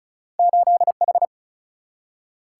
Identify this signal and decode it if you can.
Morse code, heard as a keyed tone, about 36 wpm, 700 Hz. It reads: 8H
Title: Single-Letter Single-Number 35wpm